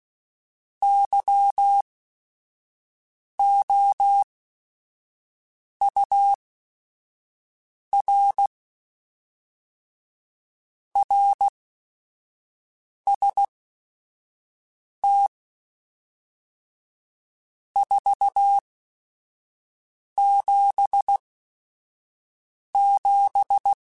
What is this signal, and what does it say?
Morse code, heard as a keyed tone, about 16 wpm, spaced as for 5 wpm, 800 Hz. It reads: YOUR RST 477
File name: RSTs / RST-mp3s / Your RSTs @5